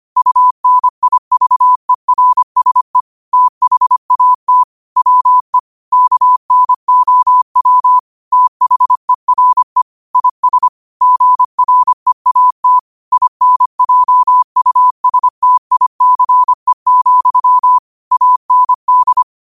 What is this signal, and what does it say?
Morse code, heard as a keyed tone, about 25 wpm, 1000 Hz. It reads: UNIVERSE THAT WE KNOW THERE IS GREAT INJUSTICE, AND
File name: qotdfiles / 25wpm